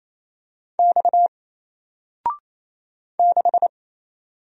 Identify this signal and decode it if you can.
Morse code, heard as a keyed tone, about 27 wpm, 700 Hz. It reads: X 6
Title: Single Letter-Number 28wpm